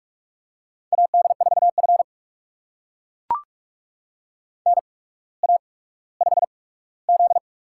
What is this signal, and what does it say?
Morse code, heard as a keyed tone, about 44 wpm, spaced as for 19 wpm, 700 Hz. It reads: AD4F NA5Z
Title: USA Call Signs - 4 Characters - 8x Character Spacing - 45wpm